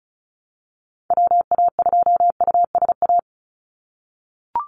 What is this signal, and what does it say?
Morse code, heard as a keyed tone, about 34 wpm, 700 Hz. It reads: WA2USA
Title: NA SOTA Activator Callsigns 35wpm